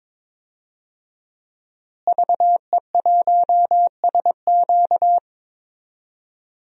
Morse code, heard as a keyed tone, about 21 wpm, 700 Hz. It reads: VE1SQ
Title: DXCC-100 Call Signs - 5 Characters - Repeated 2x - 22wpm